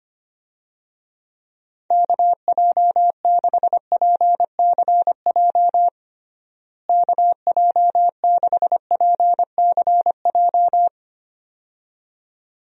Morse code, heard as a keyed tone, about 25 wpm, 700 Hz. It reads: KJ6PCJ KJ6PCJ
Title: USA Call Signs - 6 Characters - Repeated 2x - 25wpm